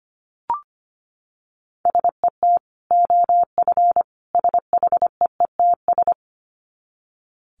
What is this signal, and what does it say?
Morse code, heard as a keyed tone, about 25 wpm, 700 Hz. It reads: SET OF SHEETS
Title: Dit Challenge 25wpm